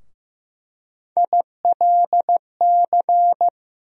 Morse code, heard as a keyed tone, about 15 wpm, 700 Hz. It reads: ILC
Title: USA Call Sign Postfixes - 3 Letters 15wpm